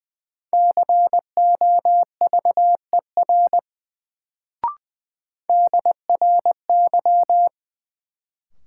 Morse code, heard as a keyed tone, about 20 wpm, 700 Hz. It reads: COVER DRY
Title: Sentences from Top 500 Words - New Word Drill - 20wpm